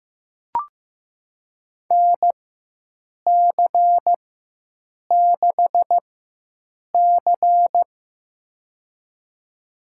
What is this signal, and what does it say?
Morse code, heard as a keyed tone, about 15 wpm, spaced as for 7 wpm, 700 Hz. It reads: NC6C
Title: USA Call Signs - 4 Characters - 4x Character Spacing - 15wpm